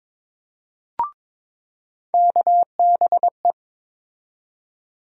Morse code, heard as a keyed tone, about 22 wpm, 700 Hz. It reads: KBE
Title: USA Call Sign Postfixes - 3 Letters 22wpm